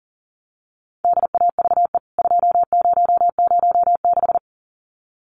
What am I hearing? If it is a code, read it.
Morse code, heard as a keyed tone, about 40 wpm, 700 Hz. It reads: DAVE 2006
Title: CWT Exchange 40wpm